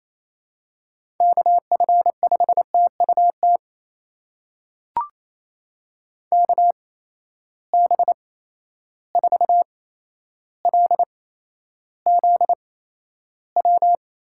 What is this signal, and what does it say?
Morse code, heard as a keyed tone, about 27 wpm, spaced as for 12 wpm, 700 Hz. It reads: KF5TUT KB4LZW
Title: USA Call Signs - 6 Characters - 8x Character Spacing - 28wpm